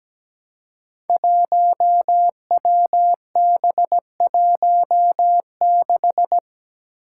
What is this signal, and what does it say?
Morse code, heard as a keyed tone, about 17 wpm, 700 Hz. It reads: 1WB16
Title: Five Challenging Letter-Number Combinations 17wpm